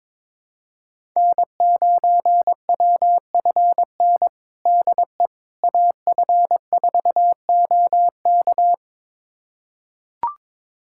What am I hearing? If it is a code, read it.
Morse code, heard as a keyed tone, about 22 wpm, 700 Hz. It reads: N9WFN DE AF4OK
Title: USA Callsign DE Callsign - 5 Characters - 22wpm